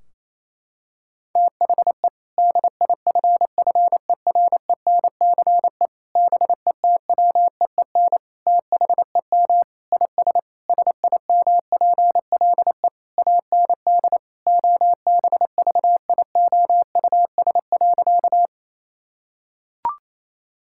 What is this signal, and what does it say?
Morse code, heard as a keyed tone, about 28 wpm, 700 Hz. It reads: THE DIFFERENCE BETWEEN THEM IS SIMPLE AND OBVIOUS.
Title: Sentences from Top 1600 Words 28wpm